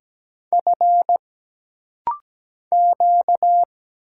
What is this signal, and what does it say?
Morse code, heard as a keyed tone, about 17 wpm, 700 Hz. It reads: F Q
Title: Single Letters - Rapid-Fire 17wpm